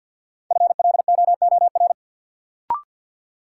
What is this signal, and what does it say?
Morse code, heard as a keyed tone, about 52 wpm, 700 Hz. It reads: FLOOR
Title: Sets of 3 Words - New Word Drill - 50wpm